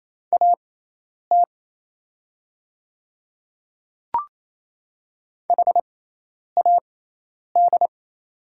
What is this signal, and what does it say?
Morse code, heard as a keyed tone, about 28 wpm, spaced as for 10 wpm, 700 Hz. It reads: AT HAD
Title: 6x Character Spacing - Top 300 Words - 28wpm